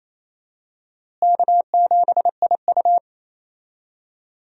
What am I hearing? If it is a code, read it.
Morse code, heard as a keyed tone, about 28 wpm, 700 Hz. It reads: K7IU
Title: USA Call Signs - 4 Characters - Repeated 2x - 28wpm